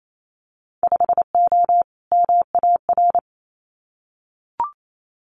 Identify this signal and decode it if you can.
Morse code, heard as a keyed tone, about 28 wpm, 700 Hz. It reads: 5O MAR
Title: Winter Field Day Exchanges Encore 28wpm